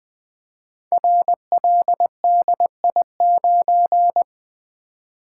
Morse code, heard as a keyed tone, about 20 wpm, 700 Hz. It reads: RLDI9
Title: Code Groups - 5 Letter:Numbers - 20wpm